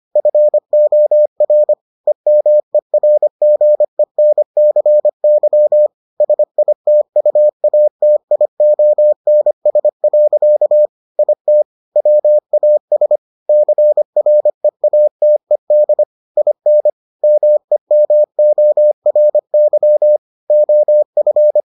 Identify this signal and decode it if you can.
Morse code, heard as a keyed tone, about 25 wpm, 600 Hz. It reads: FOR EMERGENCY SITUATIONS. IT WAS CREATED IN MEMORY OF